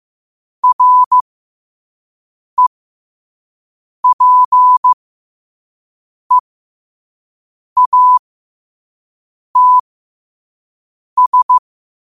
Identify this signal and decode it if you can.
Morse code, heard as a keyed tone, about 15 wpm, spaced as for 5 wpm, 1000 Hz. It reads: REPEATS
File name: qotdfiles / 5wpm